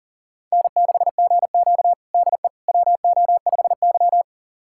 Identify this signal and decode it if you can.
Morse code, heard as a keyed tone, about 40 wpm, 700 Hz. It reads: N6GQ DE WO5Y
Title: USA Callsign DE Callsign - 4 Characters - 40wpm